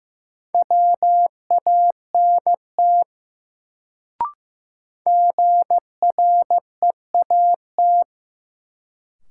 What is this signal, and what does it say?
Morse code, heard as a keyed tone, about 15 wpm, 700 Hz. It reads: WANT GREAT
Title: Sentences from Top 200 Words - New Word Drill - 15wpm